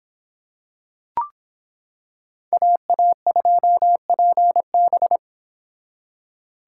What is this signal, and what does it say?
Morse code, heard as a keyed tone, about 26 wpm, 700 Hz. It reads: AA2PB